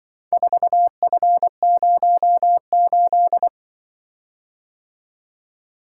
Morse code, heard as a keyed tone, about 25 wpm, 700 Hz. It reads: 4F08
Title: Speed-Racing - Four Letter-Number Combinations - 20wpm